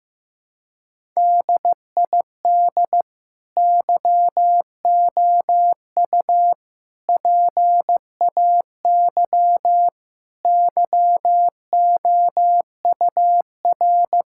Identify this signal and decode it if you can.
Morse code, heard as a keyed tone, about 15 wpm, 700 Hz. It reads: DID YOU PAY YOUR
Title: Sentences from Top 1000 Words - Review Part 5 - 15wpm